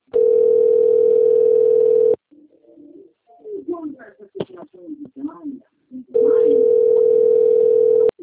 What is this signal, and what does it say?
Ringback tone (North America), 440 + 480 Hz, with speech.
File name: rg-1600-unknown- - - .76673